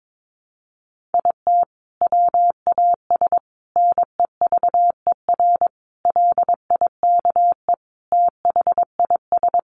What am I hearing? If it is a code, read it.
Morse code, heard as a keyed tone, about 22 wpm, 700 Hz. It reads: IT WAS NEVER LIKE THIS
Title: A Sound Of Thunder 22wpm